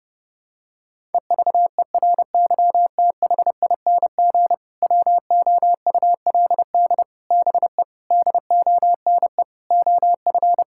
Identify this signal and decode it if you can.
Morse code, heard as a keyed tone, about 32 wpm, 700 Hz. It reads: EVERYTHING WOULD BE DONE OF
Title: Sentences from Top 1000 Words - Review Part 5 - 30wpm